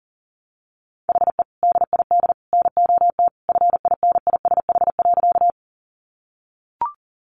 Morse code, heard as a keyed tone, about 40 wpm, 700 Hz. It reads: HE DID NOT FINISH.
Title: Sentences from Top 1200 Words 40wpm